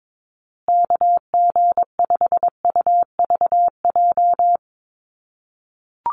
Morse code, heard as a keyed tone, about 22 wpm, 700 Hz. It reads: KG5UVJ
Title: USA Call Signs Repeated 3x 22wpm